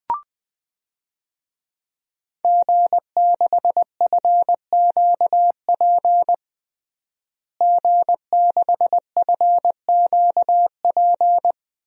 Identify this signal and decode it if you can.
Morse code, heard as a keyed tone, about 21 wpm, 700 Hz. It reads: G6FQP G6FQP
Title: DXCC-100 Call Signs - 5 Characters - Repeated 3x - 20wpm